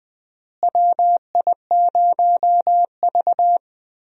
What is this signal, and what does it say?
Morse code, heard as a keyed tone, about 20 wpm, 700 Hz. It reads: WI0V
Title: USA Call Signs - 4 Characters - Repeated 2x - 20wpm